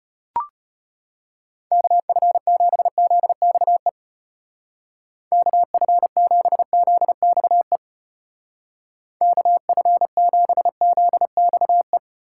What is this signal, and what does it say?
Morse code, heard as a keyed tone, about 33 wpm, 700 Hz. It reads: KF7ZXE KF7ZXE KF7ZXE